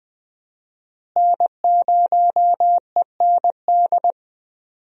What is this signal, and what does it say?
Morse code, heard as a keyed tone, about 21 wpm, 700 Hz. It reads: N0END
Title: USA Call Signs - 5 Characters - Repeated 3x - 20wpm